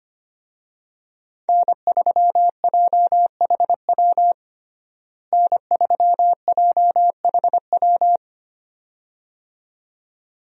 Morse code, heard as a keyed tone, about 25 wpm, 700 Hz. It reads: N3JHW N3JHW
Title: USA Call Signs - 5 Characters - Repeated 2x - 25wpm